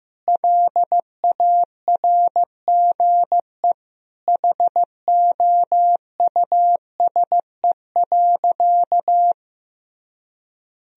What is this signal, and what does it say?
Morse code, heard as a keyed tone, about 15 wpm, 700 Hz. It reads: LARGE HOUSE.
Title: Sentences from Top 1000 Words - Review Part 1 - 15wpm